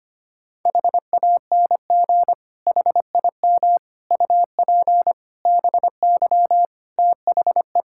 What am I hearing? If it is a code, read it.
Morse code, heard as a keyed tone, about 25 wpm, 700 Hz. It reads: HANG HIM UP BY THE
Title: Sentences from Top 1600 Words 25wpm